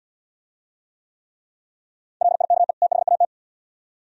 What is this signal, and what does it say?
Morse code, heard as a keyed tone, about 74 wpm, 700 Hz. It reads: LIKE THAT